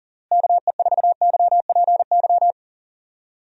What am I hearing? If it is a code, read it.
Morse code, heard as a keyed tone, about 37 wpm, 700 Hz. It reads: KE4YPY
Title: USA Call Signs - 6 Characters - Repeated 3x - 40wpm